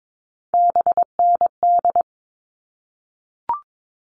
Morse code, heard as a keyed tone, about 22 wpm, 700 Hz. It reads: BND
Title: All Cut Numbers - 3-Digit 22wpm